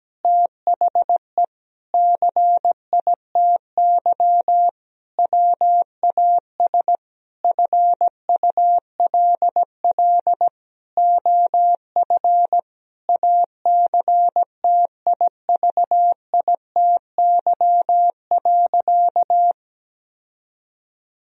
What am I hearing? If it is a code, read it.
Morse code, heard as a keyed tone, about 17 wpm, 700 Hz. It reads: THE CITY WAS FULL OF ACTIVITY.
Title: Sentences from Top 1000 Words - Review Part 4 - 17wpm